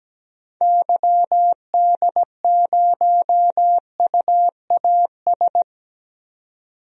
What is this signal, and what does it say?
Morse code, heard as a keyed tone, about 17 wpm, 700 Hz. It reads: YD0UAS
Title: DXCC-100 Call Signs - 6 Characters - 17wpm